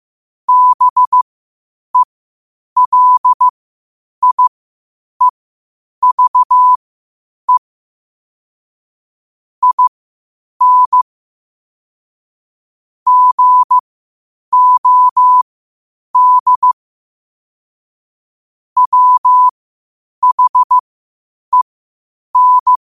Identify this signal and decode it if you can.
Morse code, heard as a keyed tone, about 15 wpm, spaced as for 8 wpm, 1000 Hz. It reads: BELIEVE IN GOD WHEN
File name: qotdfiles / 8wpm